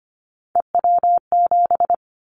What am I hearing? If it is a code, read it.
Morse code, heard as a keyed tone, about 25 wpm, 700 Hz. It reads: EW7
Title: Callsign Prefixes for Top 50 DXCC - 25wpm